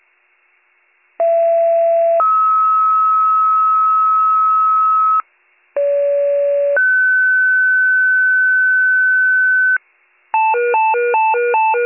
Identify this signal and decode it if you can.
Two-tone page, heard as a one-second tone, then a three-second tone, about 670 then 1290 Hz, about 570 then 1500 Hz, with noise. It is heard from 1.2 to 9.8 s.